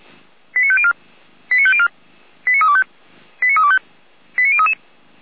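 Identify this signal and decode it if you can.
Five-tone selective call (ZVEI), heard as quick runs of five tones, tones of about 70 ms. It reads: 89504 89504 89325 89325 89033
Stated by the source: ZVEI five-tone selective call 89145 received by SDR.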